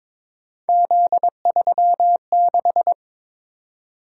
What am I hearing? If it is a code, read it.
Morse code, heard as a keyed tone, about 22 wpm, 700 Hz. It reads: Z36
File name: practice / 22wpm / Three Letter-Number Combinations - 22wpm